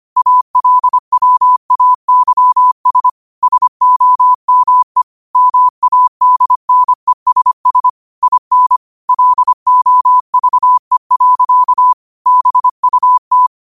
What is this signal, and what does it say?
Morse code, heard as a keyed tone, about 25 wpm, 1000 Hz. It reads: ALWAYS SOME MADNESS IN LOVE. BUT
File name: qotdfiles / 25wpm